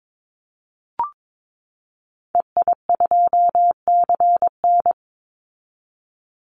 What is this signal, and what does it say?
Morse code, heard as a keyed tone, about 22 wpm, 700 Hz. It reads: EI2CN
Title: Call Signs Repeated 3x - 22wpm